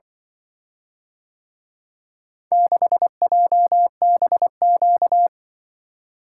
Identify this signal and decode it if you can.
Morse code, heard as a keyed tone, about 24 wpm, 700 Hz. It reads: 6JBQ
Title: Speed-Racing - Four Letter-Number Combinations - 20wpm